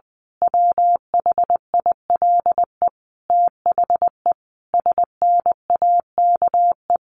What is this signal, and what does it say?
Morse code, heard as a keyed tone, about 20 wpm, 700 Hz. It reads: WHILE THE SNAKE
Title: A Sound Of Thunder 20wpm